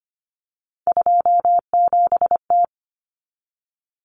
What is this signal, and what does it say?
Morse code, heard as a keyed tone, about 25 wpm, 700 Hz. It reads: 27T
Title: Common Cut Numbers - 3-Digit 25wpm